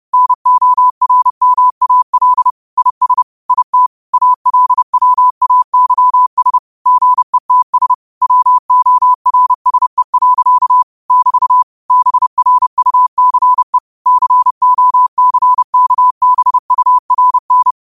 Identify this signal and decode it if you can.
Morse code, heard as a keyed tone, about 30 wpm, 1000 Hz. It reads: NORMAL IS IT ALWAYS GETS WORSE. <BT> BRUCE COCKBURN
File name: qotdfiles / 30wpm